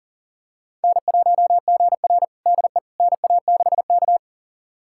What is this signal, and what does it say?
Morse code, heard as a keyed tone, about 40 wpm, 700 Hz. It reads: N1GR DE NA6K
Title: USA Callsign DE Callsign - 4 Characters - 40wpm